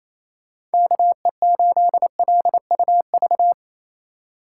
Morse code, heard as a keyed tone, about 33 wpm, 700 Hz. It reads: KE8LUV
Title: Speed Racing - USA 2x3 Call Signs 28wpm